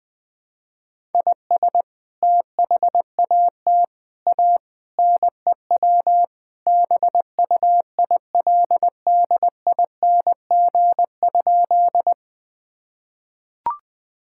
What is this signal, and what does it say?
Morse code, heard as a keyed tone, about 19 wpm, 700 Hz. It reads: IS THAT A NEW BUILDING?
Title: Sentences from Top 1000 Words - Review Part 5 - 20wpm